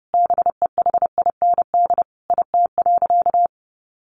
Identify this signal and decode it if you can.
Morse code, heard as a keyed tone, about 30 wpm, 700 Hz. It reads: BEHIND IT.
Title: Sentences from Top 500 Words 30wpm